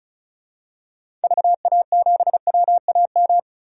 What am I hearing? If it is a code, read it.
Morse code, heard as a keyed tone, about 36 wpm, 700 Hz. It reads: VA7WAM
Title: DXCC-100 Call Signs - 6 Characters - 35wpm